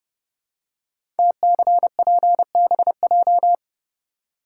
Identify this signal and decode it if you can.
Morse code, heard as a keyed tone, about 30 wpm, 700 Hz. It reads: TCPBJ
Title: Code Groups - 5 Letters - 30wpm